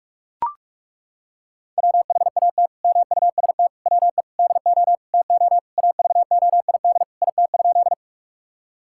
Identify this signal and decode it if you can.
Morse code, heard as a keyed tone, about 42 wpm, 700 Hz. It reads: WHAT MUST WE DO TO AVOID IT?
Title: Sentences from Top 1000 Words - Review Part 6 - 45wpm